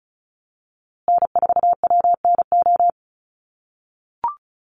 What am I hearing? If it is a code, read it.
Morse code, heard as a keyed tone, about 34 wpm, 700 Hz. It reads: N4WNO
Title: USA Call Signs Repeated 2x 35wpm